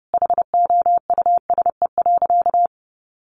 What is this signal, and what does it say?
Morse code, heard as a keyed tone, about 30 wpm, 700 Hz. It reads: HOUSE.
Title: Sentences from Top 1500 Words 30wpm